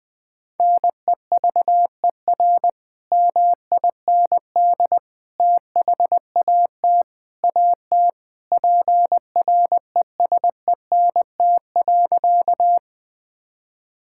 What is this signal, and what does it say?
Morse code, heard as a keyed tone, about 20 wpm, 700 Hz. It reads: NEVER MIND THAT AT PRESENT.
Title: Sentences from Top 1000 Words - Review Part 5 - 20wpm